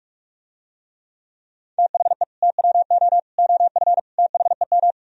Morse code, heard as a keyed tone, about 46 wpm, 700 Hz. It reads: THE TWO OF THEM